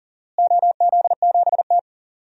Morse code, heard as a keyed tone, about 39 wpm, 700 Hz. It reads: OZ7T
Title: DXCC-100 Call Signs - 4 Characters - Repeated 2x - 40wpm